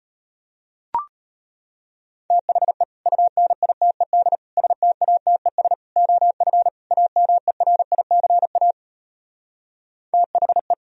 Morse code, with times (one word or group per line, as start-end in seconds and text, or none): 2.3-2.8 THE
3.1-4.4 UNITED
4.6-5.7 STATES
6.0-6.7 OF
6.9-8.7 AMERICA
10.1-10.7 THE